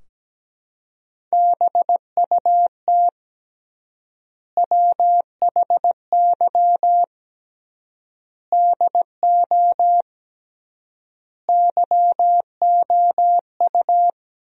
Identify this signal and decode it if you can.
Morse code, heard as a keyed tone, about 17 wpm, 700 Hz. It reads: BUT WHY DO YOU
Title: 3x Word-Spacing - Sentences from Top 200 Words - 17wpm